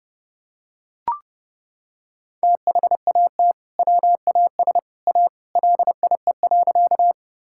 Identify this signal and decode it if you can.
Morse code, heard as a keyed tone, about 32 wpm, 700 Hz. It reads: THAT WAS A LIE.
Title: Sentences from Top 1600 Words 30wpm